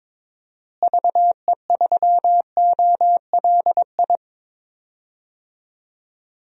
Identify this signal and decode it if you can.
Morse code, heard as a keyed tone, about 22 wpm, 700 Hz. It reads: VE3OLI
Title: DXCC-100 Call Signs - 6 Characters - Repeated 2x - 22wpm